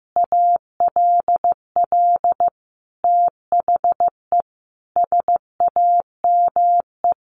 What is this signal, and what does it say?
Morse code, heard as a keyed tone, about 15 wpm, 700 Hz. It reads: ALL THE SAME
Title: Sentences from Top 600 Words 15wpm